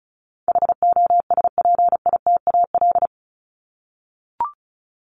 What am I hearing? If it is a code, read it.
Morse code, heard as a keyed tone, about 34 wpm, 700 Hz. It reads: HOSPITAL
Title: Top 1751-2000 Words 35wpm